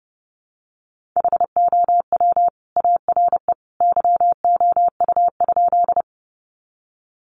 Morse code, heard as a keyed tone, about 28 wpm, 700 Hz. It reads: HOW ARE YOU?
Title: Sentences from Top 100 Words 30wpm